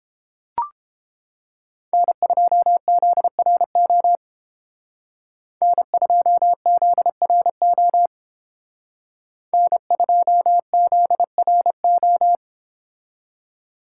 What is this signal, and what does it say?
Morse code, heard as a keyed tone, about 28 wpm, 700 Hz. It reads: N2ZRO N2ZRO N2ZRO